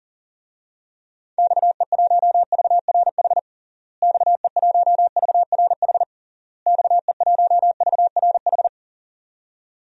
Morse code, heard as a keyed tone, about 40 wpm, 700 Hz. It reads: XE1VRH XE1VRH XE1VRH
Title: DXCC-100 Call Signs - 6 Characters - Repeated 3x - 40wpm